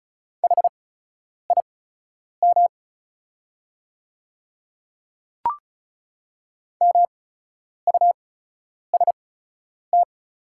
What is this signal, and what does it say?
Morse code, heard as a keyed tone, about 34 wpm, spaced as for 10 wpm, 690 Hz. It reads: HIM MUST
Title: 8x Character Spacing - Top 200 Words - 35wpm